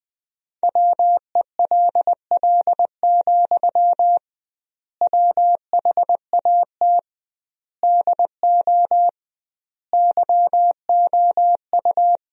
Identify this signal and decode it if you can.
Morse code, heard as a keyed tone, about 20 wpm, 700 Hz. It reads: WELL, WHAT DO YOU
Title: 2x Word-Spacing - Sentences from Top 200 Words - 20wpm